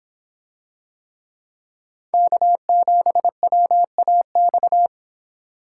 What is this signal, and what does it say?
Morse code, heard as a keyed tone, about 26 wpm, 700 Hz. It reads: K7WAX